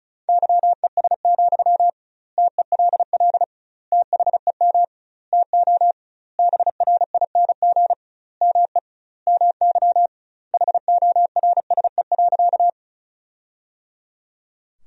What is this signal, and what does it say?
Morse code, heard as a keyed tone, about 34 wpm, 700 Hz. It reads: YES, TELL THEM TO BRING ME MY HORSE.
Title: 2x Word-Spacing - Sentences from Top 500 Words - 35wpm